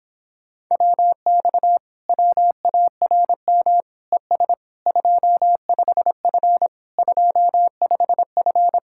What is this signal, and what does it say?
Morse code, heard as a keyed tone, about 26 wpm, 700 Hz. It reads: WX WARM ES 25F 25F